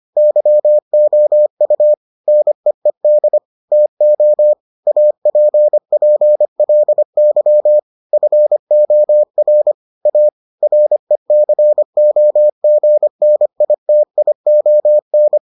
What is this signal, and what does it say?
Morse code, heard as a keyed tone, about 25 wpm, 600 Hz. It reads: YOU NEED TO APPLY FOR A RECOGNITION